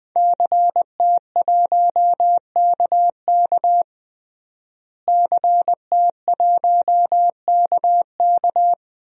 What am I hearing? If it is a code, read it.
Morse code, heard as a keyed tone, about 20 wpm, 700 Hz. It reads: CT1KK CT1KK
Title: DXCC-100 Call Signs - 5 Characters - Repeated 3x - 20wpm